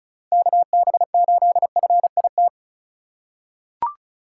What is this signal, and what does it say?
Morse code, heard as a keyed tone, about 36 wpm, 700 Hz. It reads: KB8FIT
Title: USA Call Signs - 2x Character Spacing - 35wpm